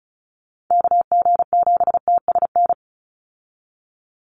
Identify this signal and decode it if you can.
Morse code, heard as a keyed tone, about 34 wpm, 700 Hz. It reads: KG7TSN
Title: USA Call Signs 35wpm